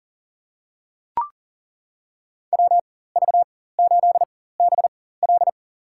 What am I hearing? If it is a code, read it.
Morse code, heard as a keyed tone, about 40 wpm, spaced as for 19 wpm, 700 Hz. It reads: WV8BL
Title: USA Call Signs - 4x Character Spacing - 40wpm